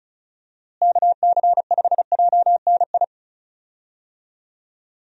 Morse code, heard as a keyed tone, about 34 wpm, 700 Hz. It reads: KC5JNI
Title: USA Call Signs - 6 Characters - Repeated 3x - 35wpm